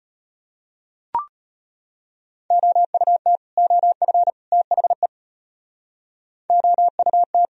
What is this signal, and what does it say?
Morse code, heard as a keyed tone, about 32 wpm, 700 Hz. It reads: OUT OF THE OUT